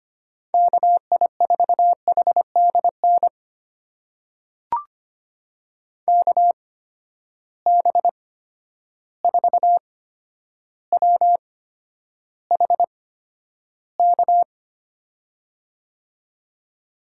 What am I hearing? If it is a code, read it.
Morse code, heard as a keyed tone, about 25 wpm, spaced as for 11 wpm, 700 Hz. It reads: KI4HDN KB4WHK
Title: USA Call Signs - 6 Characters - 8x Character Spacing - 25wpm